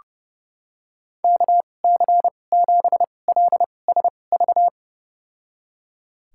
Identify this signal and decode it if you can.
Morse code, heard as a keyed tone, about 30 wpm, spaced as for 22 wpm, 700 Hz. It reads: KC7LSV
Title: USA Call Signs - 6 Characters - 2x Character Spacing - 30wpm